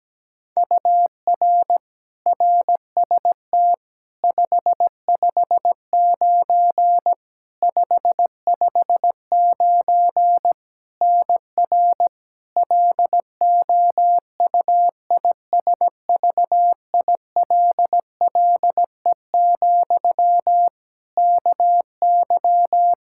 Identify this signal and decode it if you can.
Morse code, heard as a keyed tone, about 17 wpm, 700 Hz. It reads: UR RST 559 559 NR LOUISVILLE, KY
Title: Practice QSO - Exchange 1 - Beginning - 17wpm